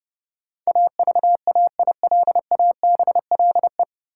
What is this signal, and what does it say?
Morse code, heard as a keyed tone, about 30 wpm, 700 Hz. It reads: AVAILABLE
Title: Sentences from Top 600 Words - New Word Drill - 30wpm